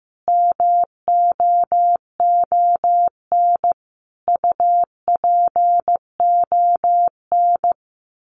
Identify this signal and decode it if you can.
Morse code, heard as a keyed tone, about 15 wpm, 700 Hz. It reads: MOON UPON
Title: Sentences from Top 1500 Words 15wpm